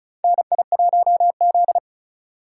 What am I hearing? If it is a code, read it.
Morse code, heard as a keyed tone, about 36 wpm, 700 Hz. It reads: NI1Z
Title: USA Call Signs - 4x Character Spacing - 35wpm